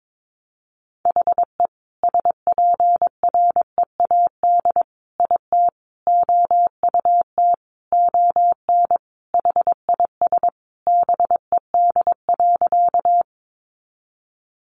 Morse code, heard as a keyed tone, about 22 wpm, 700 Hz. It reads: HE SPREAD IT OUT ON HIS BED.
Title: Sentences from Top 1200 Words 22wpm